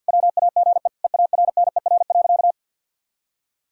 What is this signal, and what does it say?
Morse code, heard as a keyed tone, about 50 wpm, 700 Hz. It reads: WAGE EARNER.
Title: Pangram Challenge v2 - 50wpm